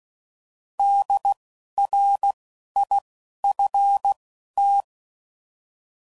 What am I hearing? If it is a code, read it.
Morse code, heard as a keyed tone, about 16 wpm, spaced as for 11 wpm, 800 Hz. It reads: DRIFT
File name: Prefixes / By First Letter / Ds